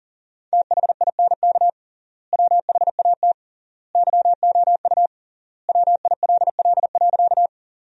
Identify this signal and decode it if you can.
Morse code, heard as a keyed tone, about 40 wpm, 700 Hz. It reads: THINK WHAT YOU WILL.
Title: 3x Word-Spacing - Sentences from Top 200 Words - 40wpm